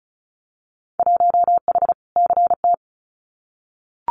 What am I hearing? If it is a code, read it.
Morse code, heard as a keyed tone, about 34 wpm, 700 Hz. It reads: 1H CT
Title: Winter Field Day Exchanges 35wpm